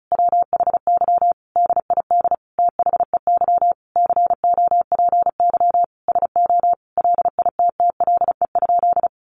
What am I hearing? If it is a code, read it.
Morse code, heard as a keyed tone, about 34 wpm, 700 Hz. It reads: WHY DID THEY COPY SO LITTLE?
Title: Sentences from Top 1400 Words 35wpm